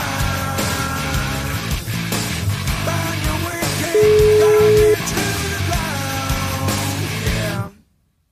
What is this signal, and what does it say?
Ringback tone (Europe, 425 Hz), with noise.